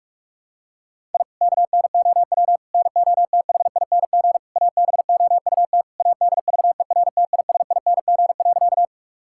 Morse code, heard as a keyed tone, about 46 wpm, 700 Hz. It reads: I KNOW NOTHING ABOUT ADVERTISING.
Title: Sentences from Top 1000 Words - Review Part 6 - 45wpm